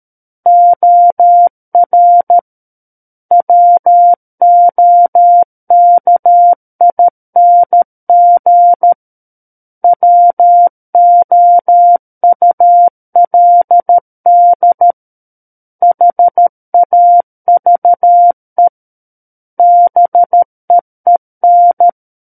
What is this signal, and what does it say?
Morse code, heard as a keyed tone, about 13 wpm, 700 Hz. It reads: OR WOKING WOULD HAVE BEEN
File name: Book One / 04-Chapter 4 - 13 WPM